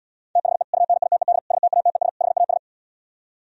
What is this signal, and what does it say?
Morse code, heard as a keyed tone, about 77 wpm, 700 Hz. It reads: THE UNITED STATES HAS